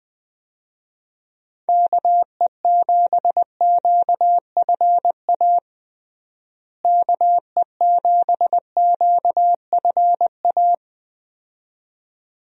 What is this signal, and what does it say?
Morse code, heard as a keyed tone, about 21 wpm, 700 Hz. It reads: KE7QFA KE7QFA
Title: USA Call Signs - 6 Characters - Repeated 2x - 20wpm